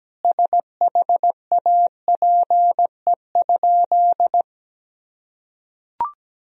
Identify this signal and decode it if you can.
Morse code, heard as a keyed tone, about 17 wpm, 700 Hz. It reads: SHAPE?
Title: Sentences from Top 1000 Words - Review Part 3 - 17wpm